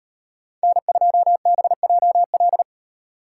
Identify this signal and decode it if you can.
Morse code, heard as a keyed tone, about 32 wpm, 700 Hz. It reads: N2BJL